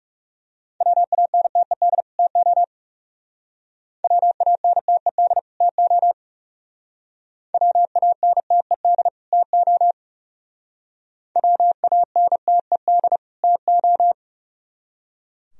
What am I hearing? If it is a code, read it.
Morse code, heard as a keyed tone, about 36 wpm, 700 Hz. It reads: WANTED TO WANTED TO WANTED TO WANTED TO
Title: Speed-Racing - Sets of 2 Words 30wpm